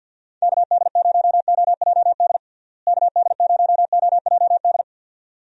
Morse code, heard as a keyed tone, about 52 wpm, 700 Hz. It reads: KD0OJD KD0OJD
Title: USA Call Signs - 6 Characters - Repeated 2x - 50wpm